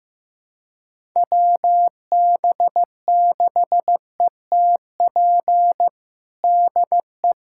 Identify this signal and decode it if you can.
Morse code, heard as a keyed tone, about 15 wpm, 700 Hz. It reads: WB6ETP DE
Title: USA Callsign DE Callsign - 6 Characters - 15wpm